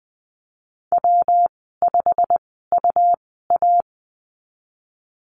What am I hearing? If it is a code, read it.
Morse code, heard as a keyed tone, about 20 wpm, spaced as for 15 wpm, 700 Hz. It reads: W5UA
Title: 147 Lesson 24g - Letter y - Call Signs 20wpm